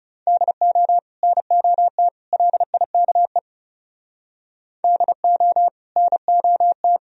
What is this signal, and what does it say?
Morse code, heard as a keyed tone, about 37 wpm, 690 Hz. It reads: DO NOT LIKE DO NOT
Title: Speed-Racing - Sets of 3 Words 30wpm